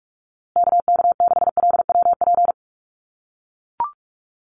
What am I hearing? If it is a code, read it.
Morse code, heard as a keyed tone, about 44 wpm, 700 Hz. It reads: KK6LWP
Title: USA Call Signs Repeated 2x 45wpm